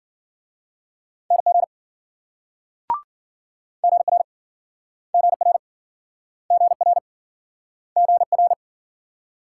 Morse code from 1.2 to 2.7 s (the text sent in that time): NG